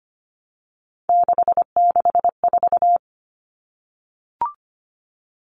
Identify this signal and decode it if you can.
Morse code, heard as a keyed tone, about 25 wpm, 700 Hz. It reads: 664
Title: Common Cut Numbers - 3-Digit 25wpm